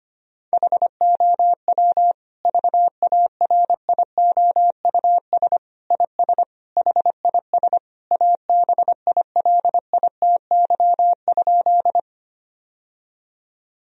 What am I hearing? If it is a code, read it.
Morse code, heard as a keyed tone, about 25 wpm, 700 Hz. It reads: HOW VARIOUS IS HIS ABILITY?
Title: Sentences from Top 1000 Words - Review Part 4 - 25wpm